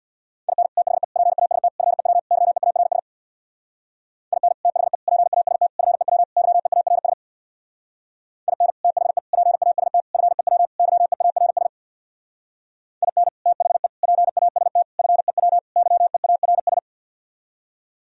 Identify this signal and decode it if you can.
Morse code, heard as a keyed tone, about 64 wpm, 690 Hz. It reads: IN THE PAST FEW YEARS IN THE PAST FEW YEARS IN THE PAST FEW YEARS IN THE PAST FEW YEARS